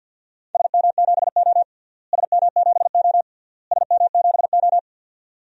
Morse code, heard as a keyed tone, about 53 wpm, 700 Hz. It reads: SM7O SM7O SM7O
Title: DXCC-100 Call Signs - 4 Characters - Repeated 3x - 50wpm